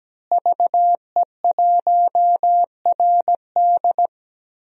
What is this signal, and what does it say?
Morse code, heard as a keyed tone, about 17 wpm, 700 Hz. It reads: VE1RD
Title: DXCC-100 Call Signs - 5 Characters - Repeated 3x - 17wpm